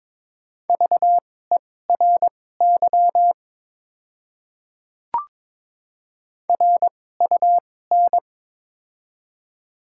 Morse code, heard as a keyed tone, about 22 wpm, spaced as for 16 wpm, 700 Hz. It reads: VERY RUN
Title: 2x Character Spacing - Top 300 Words - 22wpm